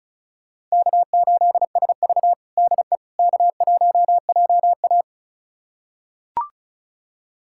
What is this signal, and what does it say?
Morse code, heard as a keyed tone, about 36 wpm, 700 Hz. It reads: K8SV DE K1JA
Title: USA Callsign DE Callsign - 4 Characters - 35wpm